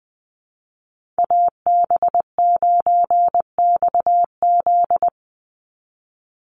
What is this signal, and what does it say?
Morse code, heard as a keyed tone, about 19 wpm, 700 Hz. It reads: AB9XZ
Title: USA Call Signs 20wpm